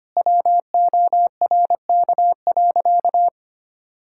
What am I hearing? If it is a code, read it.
Morse code, heard as a keyed tone, about 25 wpm, 700 Hz. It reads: WORK.
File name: Sentences with 3-Word Phrases - 25wpm